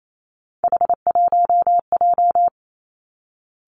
Morse code, heard as a keyed tone, about 28 wpm, 700 Hz. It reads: H1J
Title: Three Challenging Letter-Number Combinations 28wpm